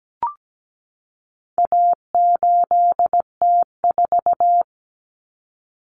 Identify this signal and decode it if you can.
Morse code, heard as a keyed tone, about 17 wpm, 700 Hz. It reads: A8T4
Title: Common Cut Numbers - 4-Digit 17wpm